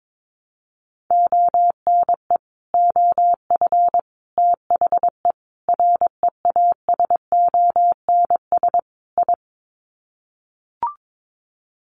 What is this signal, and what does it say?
Morse code, heard as a keyed tone, about 22 wpm, 700 Hz. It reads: ONE OF THE REASONS I
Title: Sets of 5 Words Encore2 22wpm